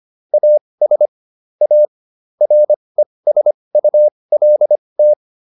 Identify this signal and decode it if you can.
Morse code, heard as a keyed tone, about 25 wpm, spaced as for 20 wpm, 600 Hz. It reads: AS A RESULT